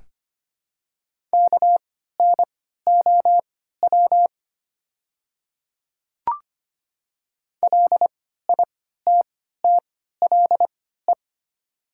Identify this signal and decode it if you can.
Morse code, heard as a keyed tone, about 25 wpm, spaced as for 14 wpm, 700 Hz. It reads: KNOW LITTLE
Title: 3x Character Spacing - Top 200 Words - 25wpm